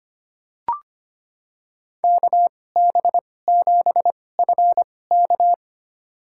Morse code, heard as a keyed tone, about 25 wpm, spaced as for 18 wpm, 700 Hz. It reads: KB7FK
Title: USA Call Signs - 5 Characters - 2x Character Spacing - 25wpm